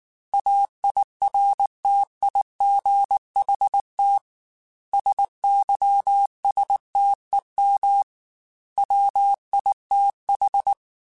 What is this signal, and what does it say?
Morse code, heard as a keyed tone, about 19 wpm, 800 Hz. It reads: AIRTIGHT SYSTEM WITH